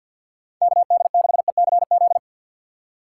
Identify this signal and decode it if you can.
Morse code, heard as a keyed tone, about 53 wpm, 700 Hz. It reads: KD6ECZ
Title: USA Call Signs - 6x Character Spacing - 50wpm